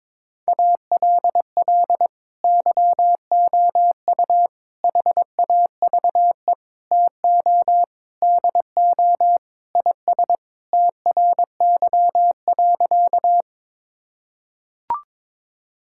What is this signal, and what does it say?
Morse code, heard as a keyed tone, about 22 wpm, 700 Hz. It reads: ALL YOU HAVE TO DO IS TRY.
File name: practice / 22wpm / Sentences with 3-Word Phrases - 22wpm